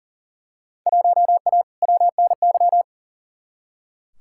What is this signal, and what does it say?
Morse code, heard as a keyed tone, about 40 wpm, 700 Hz. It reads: 1A WNY
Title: Field Day Exchanges Encore - 40wpm